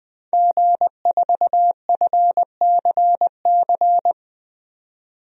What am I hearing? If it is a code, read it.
Morse code, heard as a keyed tone, about 21 wpm, 700 Hz. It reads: G4FCC
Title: DXCC-100 Call Signs - 5 Characters - 20wpm